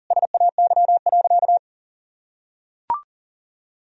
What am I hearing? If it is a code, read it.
Morse code, heard as a keyed tone, about 40 wpm, 700 Hz. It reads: SAY.
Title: Sentences from Top 1000 Words - Review Part 2 - 40wpm